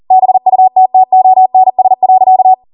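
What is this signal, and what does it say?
Morse code, heard as a keyed tone, about 40 wpm, 750 Hz. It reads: BUTTONS.
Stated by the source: Code practice at 40 wpm.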